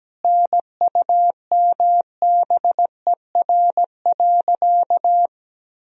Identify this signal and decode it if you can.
Morse code, heard as a keyed tone, about 17 wpm, 700 Hz. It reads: NUMBER.
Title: Sentences from Top 1000 Words - Review Part 1 - 17wpm